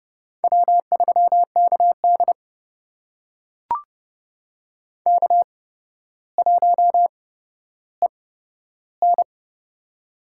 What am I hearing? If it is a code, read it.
Morse code, heard as a keyed tone, about 30 wpm, spaced as for 13 wpm, 700 Hz. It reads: W3KD K1EN
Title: USA Call Signs - 4 Characters - 8x Character Spacing - 30wpm